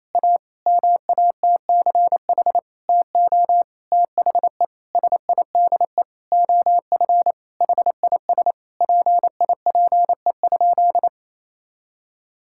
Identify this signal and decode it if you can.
Morse code, heard as a keyed tone, about 28 wpm, 700 Hz. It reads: A MATCH TO THE SIDE OF HIS PIPE?
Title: Sentences from Top 1600 Words 28wpm